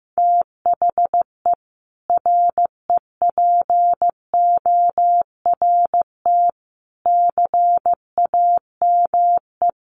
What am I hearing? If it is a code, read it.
Morse code, heard as a keyed tone, about 15 wpm, 700 Hz. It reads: THE REPORT CAME
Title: Sentences from Top 900 Words 15wpm